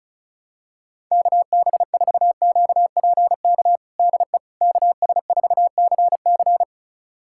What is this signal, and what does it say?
Morse code, heard as a keyed tone, about 34 wpm, 700 Hz. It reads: KB4QPK DE KS4CC
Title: USA Callsign DE Callsign 35wpm